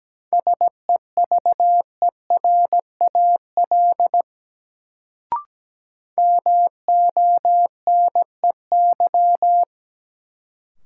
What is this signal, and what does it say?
Morse code, heard as a keyed tone, about 17 wpm, 700 Hz. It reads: SEVERAL MONEY
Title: Sentences from Top 400 Words - New Word Drill - 17wpm